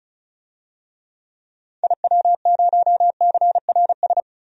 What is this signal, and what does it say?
Morse code, heard as a keyed tone, about 36 wpm, 700 Hz. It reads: IW0CRS
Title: DXCC-100 Call Signs - 6 Characters - Repeated 3x - 35wpm